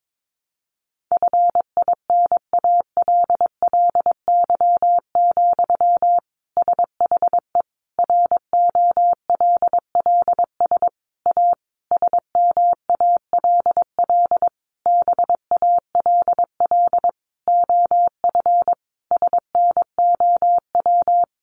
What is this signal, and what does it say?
Morse code, heard as a keyed tone, about 22 wpm, 700 Hz. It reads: FINALLY, SHE ROLLS A SMALL BALL OF SNOW
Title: Easy Phrases 22wpm